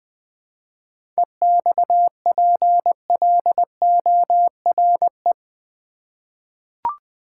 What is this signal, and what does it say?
Morse code, heard as a keyed tone, about 20 wpm, 700 Hz. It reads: EXPLORE
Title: Sentences from Top 1100 Words - New Word Drill Alt v2 - 20wpm